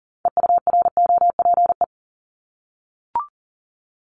Morse code, heard as a keyed tone, about 42 wpm, 700 Hz. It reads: EUROPE
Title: Multi-letter Phonemes 40wpm